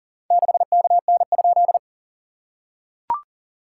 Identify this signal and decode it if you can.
Morse code, heard as a keyed tone, about 40 wpm, 690 Hz. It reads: 6KN?
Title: SOTA QSO Practice 40wpm